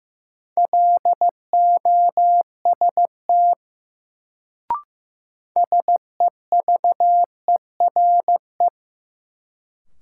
Morse code, heard as a keyed tone, about 16 wpm, 700 Hz. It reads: LOST SEVERE
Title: Sentences from Top 1400 Words - New Word Drill Alt - 15wpm